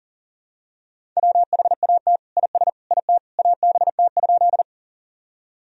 Morse code, heard as a keyed tone, about 40 wpm, 700 Hz. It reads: WHAT IS IT ABT?
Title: Sentences from Top 100 Words and Abbreviations 40wpm